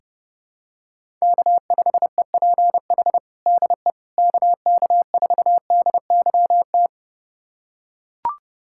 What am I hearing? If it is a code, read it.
Morse code, heard as a keyed tone, about 32 wpm, 700 Hz. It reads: K5EPH DE KK4DYT
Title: USA Callsign DE Callsign 30wpm